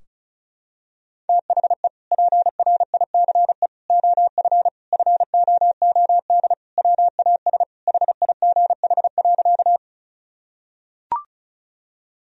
Morse code, heard as a keyed tone, about 36 wpm, 700 Hz. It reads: THE PRICE OF FOOD WAS HIGH.
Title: Sentences from Top 1000 Words - Review Part 4 - 35wpm